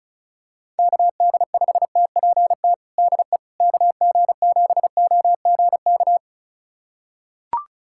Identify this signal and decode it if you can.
Morse code, heard as a keyed tone, about 36 wpm, 700 Hz. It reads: KD5TPT DE KG7OGK
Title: USA Callsign DE Callsign 35wpm